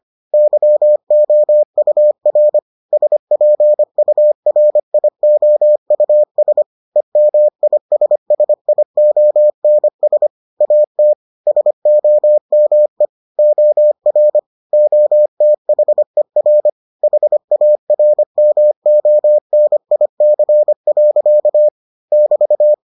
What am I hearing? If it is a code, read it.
Morse code, heard as a keyed tone, about 25 wpm, 600 Hz. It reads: YOUR SPURIOUS EMISSIONS AT SOME OR OTHER HARMONIC. <BT>